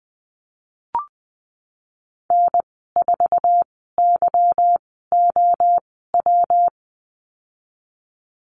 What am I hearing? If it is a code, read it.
Morse code, heard as a keyed tone, about 20 wpm, spaced as for 14 wpm, 700 Hz. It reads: N4YOW
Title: 147 Lesson 24g - Letter y - Call Signs 20wpm